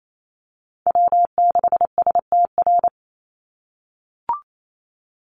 Morse code, heard as a keyed tone, about 28 wpm, 700 Hz. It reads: W6STR
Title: NA SOTA Chaser Callsigns 28wpm